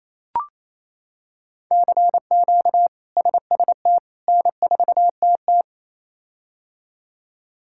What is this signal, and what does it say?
Morse code, heard as a keyed tone, about 28 wpm, 700 Hz. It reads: CQ SST N4TT
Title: K1USN SST QSO Practice - 28wpm